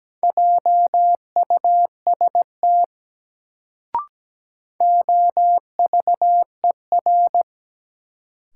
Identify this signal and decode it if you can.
Morse code, heard as a keyed tone, about 17 wpm, 700 Hz. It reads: JUST OVER
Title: Sets of 2 Words - New Word Drill - 17wpm